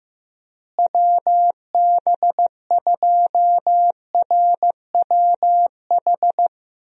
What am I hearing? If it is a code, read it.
Morse code, heard as a keyed tone, about 15 wpm, 700 Hz. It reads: WB2RWH
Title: USA Call Signs - 6 Characters - Repeated 3x - 15wpm